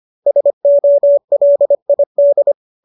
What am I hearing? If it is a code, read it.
Morse code, heard as a keyed tone, about 25 wpm, 590 Hz. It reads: SOLID
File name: qso206 25